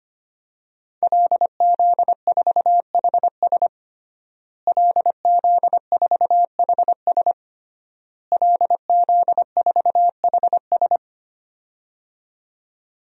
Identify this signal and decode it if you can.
Morse code, heard as a keyed tone, about 25 wpm, 700 Hz. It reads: LZ4HS LZ4HS LZ4HS
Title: DXCC-100 Call Signs - 5 Characters - Repeated 3x - 25wpm